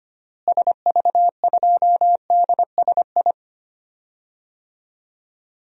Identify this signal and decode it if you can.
Morse code, heard as a keyed tone, about 25 wpm, 700 Hz. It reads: SV2DSI
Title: DXCC-100 Call Signs - 6 Characters - Repeated 3x - 25wpm